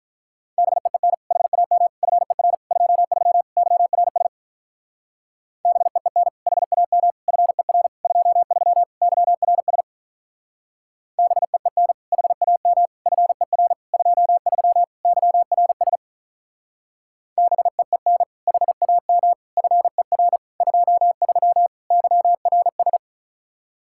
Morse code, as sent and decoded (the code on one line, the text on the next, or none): -... . -. / .... .- -- / ..-. .-. / ..--- ...-- / -.-- .-. ... / -... . -. / .... .- -- / ..-. .-. / ..--- ...-- / -.-- .-. ... / -... . -. / .... .- -- / ..-. .-. / ..--- ...-- / -.-- .-. ... / -... . -. / .... .- -- / ..-. .-. / ..--- ...-- / -.-- .-. ...
BEEN HAM FER 23 YRS BEEN HAM FER 23 YRS BEEN HAM FER 23 YRS BEEN HAM FER 23 YRS